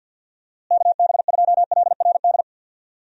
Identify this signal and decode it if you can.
Morse code, heard as a keyed tone, about 50 wpm, 700 Hz. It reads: KB2LRD
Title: USA Call Signs - 6 Characters - Repeated 3x - 50wpm